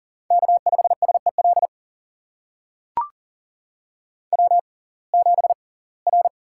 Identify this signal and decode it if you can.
Morse code, heard as a keyed tone, about 40 wpm, spaced as for 24 wpm, 700 Hz. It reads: K5SEL W7R
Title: USA Call Signs - 5 Characters - 6x Character Spacing - 40wpm